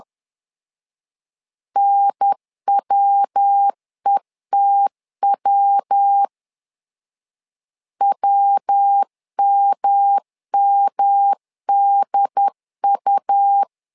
Morse code, heard as a keyed tone, about 10 wpm, 800 Hz. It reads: NWETW WMMDU
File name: LEZIONI / 3 MWRO / 3 MW 10wpm 1959